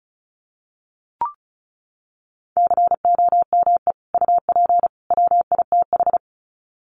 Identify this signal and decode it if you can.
Morse code, heard as a keyed tone, about 34 wpm, 700 Hz. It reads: COME UP WITH
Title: Sets of 3 Words 35wpm